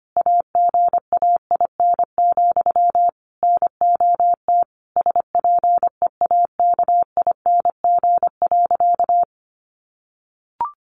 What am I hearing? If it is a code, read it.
Morse code, heard as a keyed tone, about 25 wpm, 700 Hz. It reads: AGAIN, NOT SPEAKING.
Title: A Sound Of Thunder 25wpm